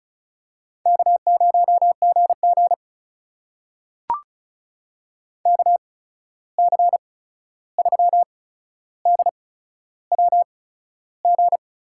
Morse code, heard as a keyed tone, about 36 wpm, spaced as for 13 wpm, 700 Hz. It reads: K0GG KC3DWG
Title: USA Call Signs - 8x Character Spacing - 35wpm